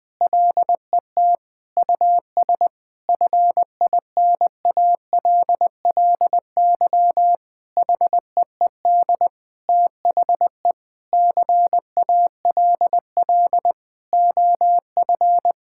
Morse code, heard as a keyed tone, about 20 wpm, 700 Hz. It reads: LET US FINALLY HEED THE CALL OF
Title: Sentences with 8-Word Phrases - 20wpm